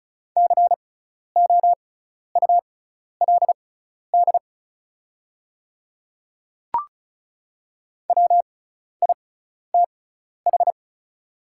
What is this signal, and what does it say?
Morse code, heard as a keyed tone, about 36 wpm, spaced as for 12 wpm, 700 Hz. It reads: COULD WITH
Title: 6x Character Spacing - Top 100 Words - 35wpm